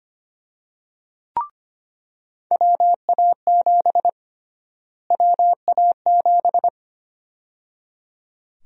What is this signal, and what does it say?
Morse code, heard as a keyed tone, about 25 wpm, 700 Hz. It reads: WA7 WA7
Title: USA Call Sign Prefixes - Repeated 2x - 25wpm